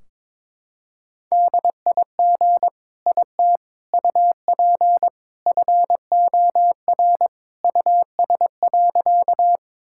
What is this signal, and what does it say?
Morse code, heard as a keyed tone, about 21 wpm, 700 Hz. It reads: DIG IT UP FOR US.
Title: Sentences from Top 1600 Words 22wpm